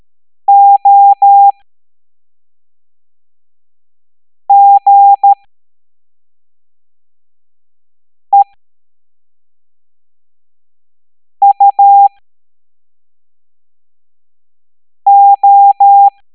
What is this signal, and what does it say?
Morse code, heard as a keyed tone, about 13 wpm, spaced as for 3 wpm, 800 Hz. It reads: OGEUO